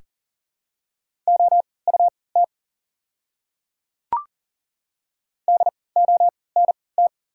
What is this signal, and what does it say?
Morse code, heard as a keyed tone, about 40 wpm, spaced as for 23 wpm, 700 Hz. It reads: OUT DONT
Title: 3x Character Spacing - Top 300 Words - 40wpm